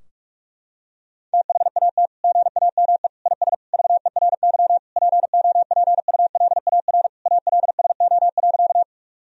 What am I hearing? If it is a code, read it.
Morse code, heard as a keyed tone, about 46 wpm, 700 Hz. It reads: THAT GAME IS VERY POPULAR ALSO.
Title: Sentences from Top 1000 Words - Review Part 3 - 45wpm